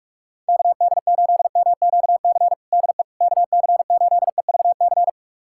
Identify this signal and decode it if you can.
Morse code, heard as a keyed tone, about 46 wpm, 700 Hz. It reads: KD8MQC DE KC8EVC
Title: USA Callsign DE Callsign 45wpm